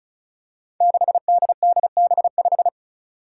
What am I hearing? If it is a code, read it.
Morse code, heard as a keyed tone, about 34 wpm, 700 Hz. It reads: 6DDB5
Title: Five Challenging Letter-Number Combinations 35wpm